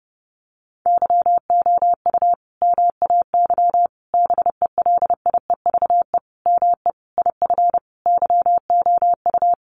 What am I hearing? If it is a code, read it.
Morse code, heard as a keyed tone, about 30 wpm, 700 Hz. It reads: YOU MAY BELIEVE ME IF YOU
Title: Sentences from Top 1000 Words 30wpm